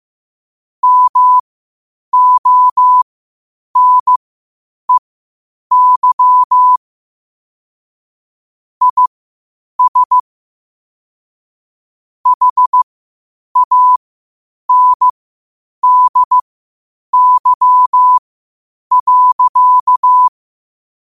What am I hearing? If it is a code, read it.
Morse code, heard as a keyed tone, about 15 wpm, spaced as for 8 wpm, 1000 Hz. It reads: MONEY IS HANDY.
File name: qotdfiles / 8wpm